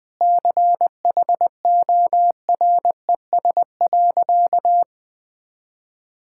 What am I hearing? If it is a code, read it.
Morse code, heard as a keyed tone, about 20 wpm, 700 Hz. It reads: CHORES.
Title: Sentences with 5-Word Phrases - 20wpm